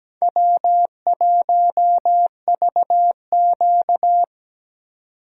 Morse code, heard as a keyed tone, about 20 wpm, 700 Hz. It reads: W1VQ
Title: Speed Racing - USA Call Signs 17wpm